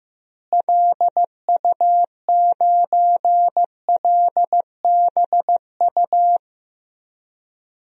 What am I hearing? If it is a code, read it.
Morse code, heard as a keyed tone, about 15 wpm, 700 Hz. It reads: LU9LBU
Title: DXCC-100 Call Signs - 6 Characters - Repeated 2x - 15wpm